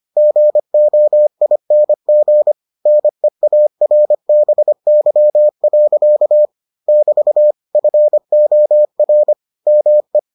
Morse code, heard as a keyed tone, about 25 wpm, 600 Hz. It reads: GOING NEARBY. <BT> FOR ME